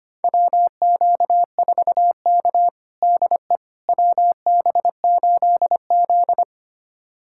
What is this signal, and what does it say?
Morse code, heard as a keyed tone, about 25 wpm, 700 Hz. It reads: WQ4K DE WB8Z
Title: USA Callsign DE Callsign - 4 Characters - 25wpm